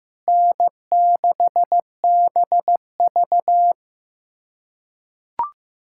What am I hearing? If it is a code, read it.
Morse code, heard as a keyed tone, about 16 wpm, 700 Hz. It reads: N6BV
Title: USA Callsign DE Callsign - 4 Characters - 15wpm